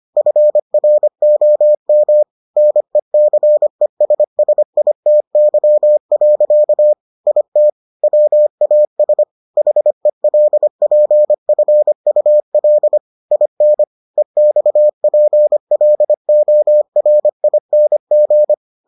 Morse code, heard as a keyed tone, about 25 wpm, 600 Hz. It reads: FROM NECESSITY. IT WAS HELPFUL IN EXPLORING